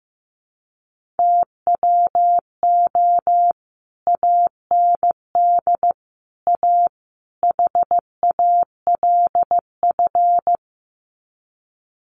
Morse code, heard as a keyed tone, about 15 wpm, 700 Hz. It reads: TWO AND A HALF
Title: Sets of 4 Words 15wpm